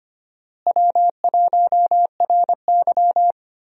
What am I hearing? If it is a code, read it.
Morse code, heard as a keyed tone, about 25 wpm, 700 Hz. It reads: W1RY
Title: USA Call Signs - 4 Characters - Repeated 3x - 25wpm